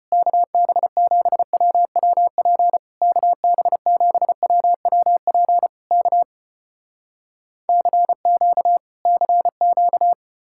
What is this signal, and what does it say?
Morse code, heard as a keyed tone, about 34 wpm, 700 Hz. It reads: KB7WWP KB7WWP K CQ CQ